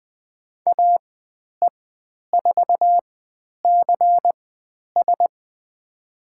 Morse code, heard as a keyed tone, about 20 wpm, spaced as for 10 wpm, 700 Hz. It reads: AE4CS
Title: 109 Lesson 18g - Letter m - Call Signs 20wpm